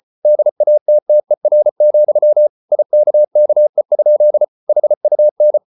Morse code, heard as a keyed tone, about 34 wpm, 600 Hz. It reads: DATTER, IKKE? HUN
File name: Ferie QRQ 34wpm